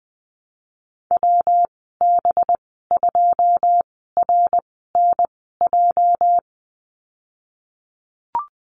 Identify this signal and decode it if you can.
Morse code, heard as a keyed tone, about 20 wpm, spaced as for 14 wpm, 700 Hz. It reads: WB2RNJ
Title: 219 Lesson 35g - Number 0 - Call Signs 20wpm